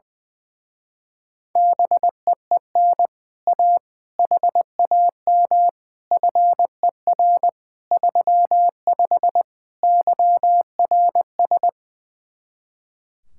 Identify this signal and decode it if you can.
Morse code, heard as a keyed tone, about 20 wpm, 700 Hz. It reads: BEEN A HAM FER 35 YRS
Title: Speed-Racing Common QSO Phrases - 20wpm